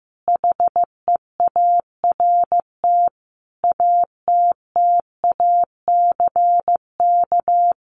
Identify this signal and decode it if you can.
Morse code, heard as a keyed tone, about 15 wpm, 700 Hz. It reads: HEART ATTACK
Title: Sets of 5 Words Encore2 15wpm